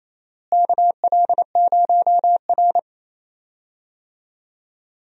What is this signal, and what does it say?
Morse code, heard as a keyed tone, about 28 wpm, 700 Hz. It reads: KL0R
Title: USA Call Signs - 4 Characters - Repeated 2x - 28wpm